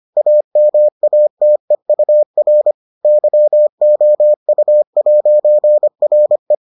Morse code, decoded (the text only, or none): AMATEUR YOU'RE